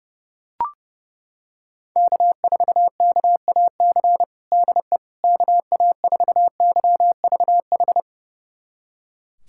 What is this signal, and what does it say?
Morse code, heard as a keyed tone, about 30 wpm, 700 Hz. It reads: K4KAC DE KA4YVH
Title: USA Callsign DE Callsign 30wpm